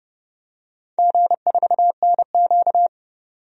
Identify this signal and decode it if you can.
Morse code, heard as a keyed tone, about 31 wpm, 700 Hz. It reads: G4NQ
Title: DXCC-100 Call Signs - 4 Characters - 30wpm